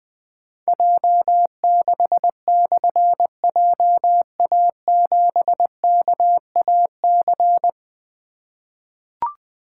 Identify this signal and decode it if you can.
Morse code, heard as a keyed tone, about 21 wpm, 700 Hz. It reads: J6/JA7KAC
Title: Difficult Call Signs 20wpm